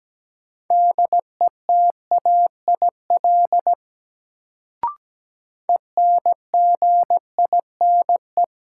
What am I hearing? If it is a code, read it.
Morse code, heard as a keyed tone, about 17 wpm, 700 Hz. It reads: DETAIL ENGINE
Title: Sentences from Top 1500 Words - New Word Drill - 17wpm